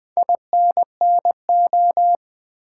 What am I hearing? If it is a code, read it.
Morse code, heard as a keyed tone, about 20 wpm, 700 Hz. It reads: INNO
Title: Build A Word - 2 Letters At A Time - 8-Letter Words - 20wpm